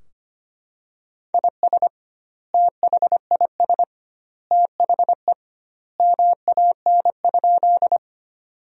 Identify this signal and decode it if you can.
Morse code, heard as a keyed tone, about 25 wpm, 700 Hz. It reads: IS THIS THE MAN?
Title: 2x Word-Spacing - Sentences from Top 200 Words - 25wpm